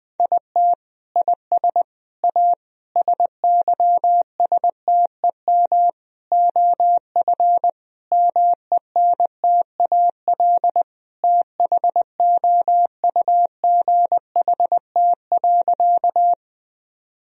Morse code, heard as a keyed tone, about 21 wpm, 700 Hz. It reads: IT IS A SYSTEM OF MENTAL THOUGHT.
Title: Sentences from Top 1000 Words - Review Part 4 - 20wpm